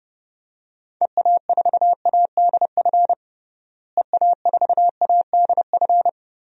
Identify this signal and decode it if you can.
Morse code, heard as a keyed tone, about 30 wpm, 700 Hz. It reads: EA4ADF EA4ADF
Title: DXCC-100 Call Signs - 6 Characters - Repeated 2x - 30wpm